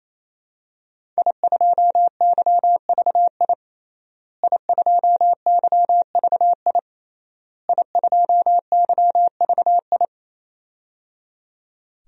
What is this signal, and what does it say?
Morse code, heard as a keyed tone, about 28 wpm, 700 Hz. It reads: I2YVI I2YVI I2YVI
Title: DXCC-100 Call Signs - 5 Characters - Repeated 3x - 28wpm